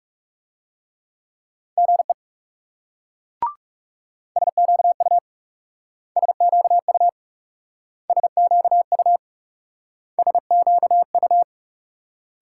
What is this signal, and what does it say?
Morse code, heard as a keyed tone, about 38 wpm, 700 Hz. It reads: GE SQU SQU SQU SQU